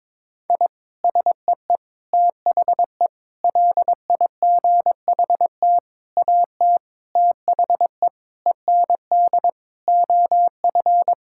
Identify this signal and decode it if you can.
Morse code, heard as a keyed tone, about 22 wpm, 700 Hz. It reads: I SEE THE LIGHT AT THE END OF
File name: practice / 22wpm / Sentences with 8-Word Phrases - 22wpm